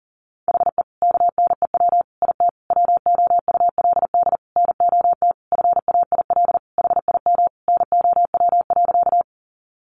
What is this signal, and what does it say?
Morse code, heard as a keyed tone, about 40 wpm, 700 Hz. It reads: HE KNEW IT WOULD NOT FAIL HIM NOW.
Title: Sentences from Top 1100 Words 40wpm